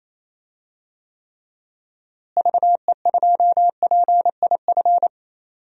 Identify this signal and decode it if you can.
Morse code, heard as a keyed tone, about 28 wpm, 700 Hz. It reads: VE2PIF
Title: DXCC-100 Call Signs - 6 Characters - Repeated 2x - 28wpm